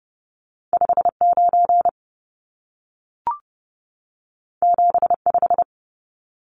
Morse code, heard as a keyed tone, about 30 wpm, 700 Hz. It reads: 59 75
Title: Numbers - 2-Digit 30wpm